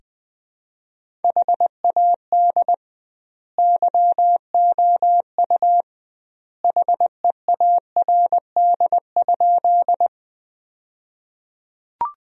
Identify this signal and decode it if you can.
Morse code, heard as a keyed tone, about 20 wpm, 700 Hz. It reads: HAD YOU HEARD?
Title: 2x Word-Spacing - Sentences from Top 400 Words - 20wpm